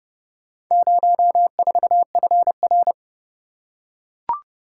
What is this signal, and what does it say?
Morse code, heard as a keyed tone, about 27 wpm, 700 Hz. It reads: 04FR